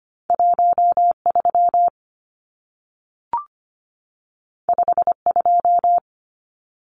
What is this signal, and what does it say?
Morse code, heard as a keyed tone, about 25 wpm, 700 Hz. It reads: 13 52
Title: Numbers - 2-Digit 25wpm